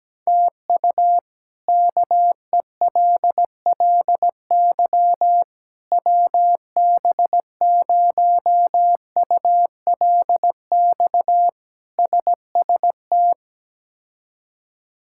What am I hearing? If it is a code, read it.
Morse code, heard as a keyed tone, about 17 wpm, 700 Hz. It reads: TU KELLY WB0ULX SST
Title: K1USN SST QSO Practice - 17wpm